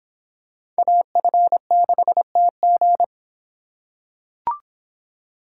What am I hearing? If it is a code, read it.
Morse code, heard as a keyed tone, about 26 wpm, 700 Hz. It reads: AF6TG